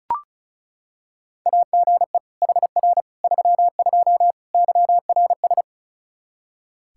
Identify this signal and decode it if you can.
Morse code, heard as a keyed tone, about 36 wpm, 700 Hz. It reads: AGE HR 32 YRS
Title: Practice QSO - Exchange 3 - More Info - 35wpm